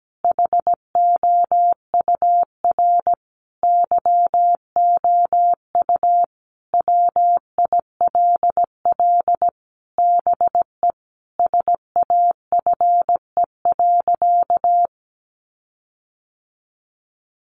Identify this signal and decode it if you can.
Morse code, heard as a keyed tone, about 17 wpm, 700 Hz. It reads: HOUR YOU WILL BE SAFE.
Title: Sentences from Top 1300 Words 17wpm